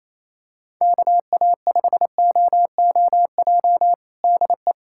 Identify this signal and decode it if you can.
Morse code, heard as a keyed tone, about 28 wpm, 700 Hz. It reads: KA5OOJ DE
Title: USA Callsign DE Callsign 28wpm